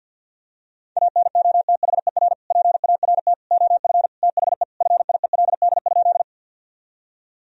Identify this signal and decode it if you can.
Morse code, heard as a keyed tone, about 50 wpm, 700 Hz. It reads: ANOTHER PART OF THE FIELD?
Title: Sentences from Top 1000 Words - Review Part 2 - 50wpm